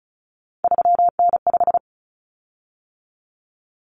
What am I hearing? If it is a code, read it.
Morse code, heard as a keyed tone, about 34 wpm, 700 Hz. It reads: 3N5
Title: Common Cut Numbers - 3-Digit 35wpm